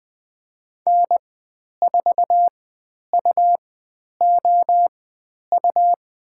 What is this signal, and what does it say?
Morse code, heard as a keyed tone, about 20 wpm, spaced as for 10 wpm, 700 Hz. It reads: N4UOU
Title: 109 Lesson 18g - Letter m - Call Signs 20wpm